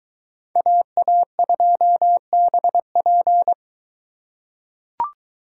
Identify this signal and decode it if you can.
Morse code, heard as a keyed tone, about 22 wpm, 700 Hz. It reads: AA2BP